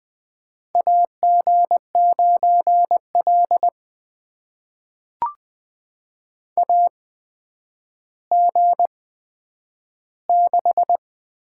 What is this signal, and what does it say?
Morse code, heard as a keyed tone, about 20 wpm, spaced as for 10 wpm, 700 Hz. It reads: AG9L AG6
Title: USA Call Signs - 4 Characters - 8x Character Spacing - 20wpm